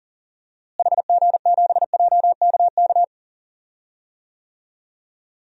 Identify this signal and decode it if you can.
Morse code, heard as a keyed tone, about 37 wpm, 700 Hz. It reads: HG7JKK
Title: DXCC-100 Call Signs - 6 Characters - Repeated 3x - 40wpm